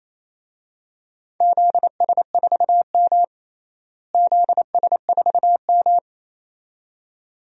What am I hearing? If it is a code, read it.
Morse code, heard as a keyed tone, about 28 wpm, 700 Hz. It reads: ZS4M ZS4M
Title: DXCC-100 Call Signs - 4 Characters - Repeated 2x - 28wpm